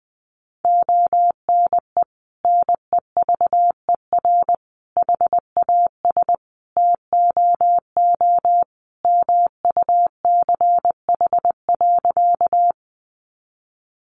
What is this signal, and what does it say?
Morse code, heard as a keyed tone, about 20 wpm, 700 Hz. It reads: ONE NEVER HAS TOO MUCH.
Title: Sentences from Top 300 Words 20wpm